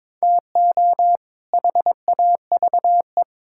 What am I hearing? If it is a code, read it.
Morse code, heard as a keyed tone, about 22 wpm, 700 Hz. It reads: TO HAVE
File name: Build A Phrase - 2-Word Phrases - 22wpm